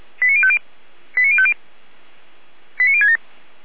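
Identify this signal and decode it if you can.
Five-tone selective call (ZVEI), heard as quick runs of five tones, tones of about 70 ms. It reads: 89050 89050 89067
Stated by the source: ZVEI five-tone selective call received by SDR.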